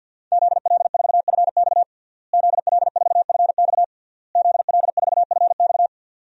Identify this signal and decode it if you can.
Morse code, heard as a keyed tone, about 52 wpm, 700 Hz. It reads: ZL4FX ZL4FX ZL4FX
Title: DXCC-100 Call Signs - 5 Characters - Repeated 3x - 50wpm